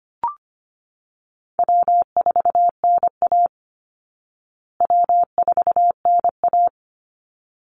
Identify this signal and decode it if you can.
Morse code, heard as a keyed tone, about 25 wpm, 700 Hz. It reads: W4NA W4NA
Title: Easy Call Signs Repeated 2x - 25wpm